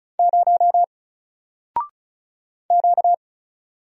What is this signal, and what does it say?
Morse code, heard as a keyed tone, about 36 wpm, 700 Hz. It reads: 0 Q
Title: Single Letter-Number Alt 35wpm